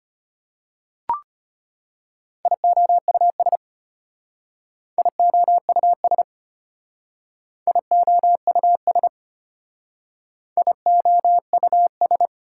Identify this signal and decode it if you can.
Morse code, heard as a keyed tone, about 30 wpm, 690 Hz. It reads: IOUS IOUS IOUS IOUS
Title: Speed-Racing Prefixes and Suffixes 25wpm